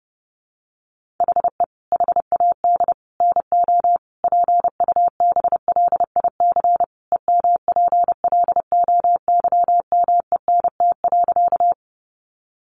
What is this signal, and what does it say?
Morse code, heard as a keyed tone, about 30 wpm, 700 Hz. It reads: HE HAD NO PUBLIC EMPLOYMENT.
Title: Sentences from Top 1500 Words 30wpm